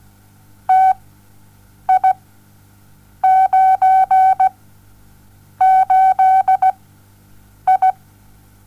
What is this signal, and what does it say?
Morse code, heard as a keyed tone, about 17 wpm, spaced as for 7 wpm, 760 Hz. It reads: TI98I
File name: Onlinelehrgang / morsen / DJ4UF Morsen07